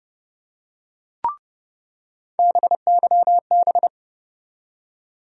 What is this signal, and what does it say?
Morse code, heard as a keyed tone, about 30 wpm, 700 Hz. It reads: BYB
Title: USA Call Sign Postfixes - 3 Letters 30wpm